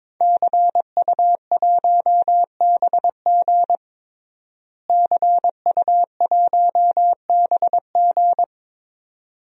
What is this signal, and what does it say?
Morse code, heard as a keyed tone, about 22 wpm, 700 Hz. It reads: CU1BG CU1BG
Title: DXCC-100 Call Signs - 5 Characters - Repeated 2x - 22wpm